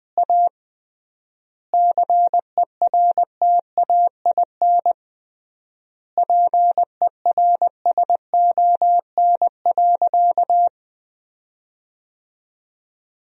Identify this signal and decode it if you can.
Morse code, heard as a keyed tone, about 20 wpm, 700 Hz. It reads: A CERTAIN PERSON.
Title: 3x Word-Spacing - Sentences from Top 500 Words - 20wpm